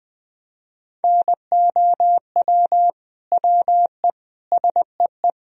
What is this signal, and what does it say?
Morse code, heard as a keyed tone, about 20 wpm, 700 Hz. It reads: NOW WE SEE
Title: Sentences from Top 1000 Words - Review Part 1 - 20wpm